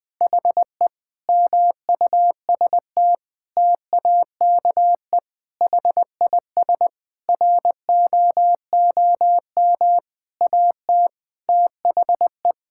Morse code, heard as a keyed tone, about 20 wpm, 700 Hz. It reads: HE MUST TAKE HIS ROOM AT THE
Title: Sentences from Top 1600 Words 20wpm